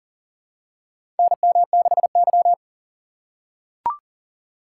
Morse code, heard as a keyed tone, about 40 wpm, 700 Hz. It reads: NM6Y
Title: USA Call Signs - 4 Characters - 6x Character Spacing - 40wpm